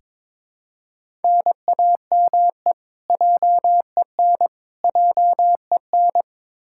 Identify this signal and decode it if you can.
Morse code, heard as a keyed tone, about 22 wpm, 700 Hz. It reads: NAME JEN JEN
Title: K1USN SST QSO Practice 22wpm